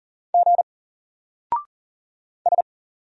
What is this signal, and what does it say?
Morse code, heard as a keyed tone, about 38 wpm, 700 Hz. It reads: G S
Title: Single Character (Letter:Number:Punctuation:Prosign) 40wpm